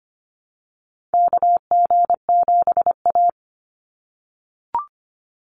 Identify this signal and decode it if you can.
Morse code, heard as a keyed tone, about 25 wpm, 700 Hz. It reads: KG7A
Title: Call Signs - 25wpm